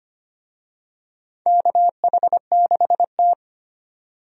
Morse code, heard as a keyed tone, about 25 wpm, 700 Hz. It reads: KH6T
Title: USA Call Signs - 4 Characters - 25wpm